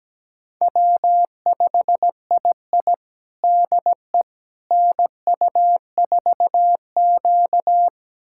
Morse code, heard as a keyed tone, about 17 wpm, 700 Hz. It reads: W5II DE NU4Q
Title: USA Callsign DE Callsign - 4 Characters - 17wpm